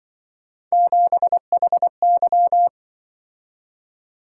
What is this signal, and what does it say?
Morse code, heard as a keyed tone, about 24 wpm, 700 Hz. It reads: 7HY
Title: Speed-Racing - Three Letter-Number Combinations - 20wpm